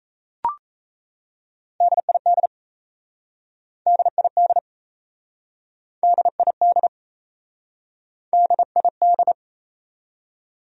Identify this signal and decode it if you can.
Morse code, heard as a keyed tone, about 38 wpm, 700 Hz. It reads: DID DID DID DID